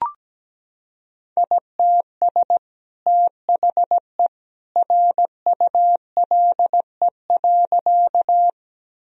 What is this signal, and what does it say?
Morse code, heard as a keyed tone, about 17 wpm, 700 Hz. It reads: ITS THE RULE.
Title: Sentences from Top 1000 Words - Review Part 2 - 17wpm